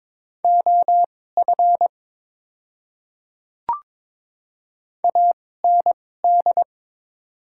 Morse code, heard as a keyed tone, about 23 wpm, spaced as for 16 wpm, 700 Hz. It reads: OF AND
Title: 2x Character Spacing - Top 100 Words - 22wpm